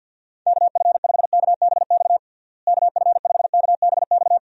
Morse code, heard as a keyed tone, about 48 wpm, 700 Hz. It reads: KF5KBX KF5KBX
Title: USA Call Signs - 6 Characters - Repeated 2x - 50wpm